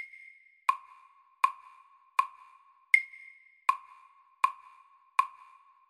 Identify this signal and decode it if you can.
There is no code.